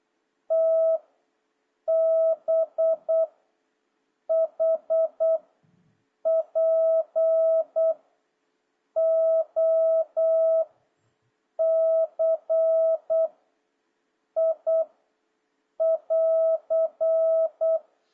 Morse code, heard as a keyed tone, about 8 wpm, 650 Hz. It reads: T B H P O C I <AR>
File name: jh8cbh / 2026cw